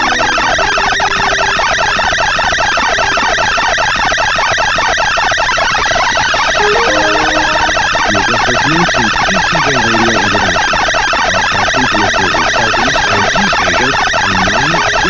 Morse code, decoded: none